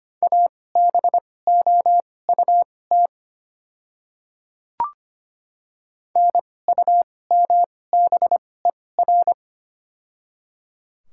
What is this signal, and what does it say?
Morse code, heard as a keyed tone, about 25 wpm, spaced as for 18 wpm, 690 Hz. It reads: ABOUT NUMBER
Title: 2x Character Spacing - Top 100 Words - 25wpm